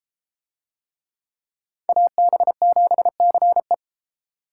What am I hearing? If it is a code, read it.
Morse code, heard as a keyed tone, about 32 wpm, 700 Hz. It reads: AB7CE